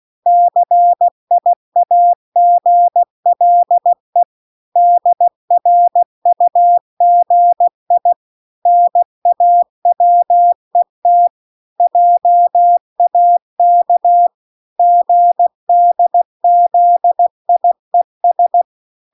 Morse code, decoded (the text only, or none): CIAGLE DRUGI NAWET JAK GDZIES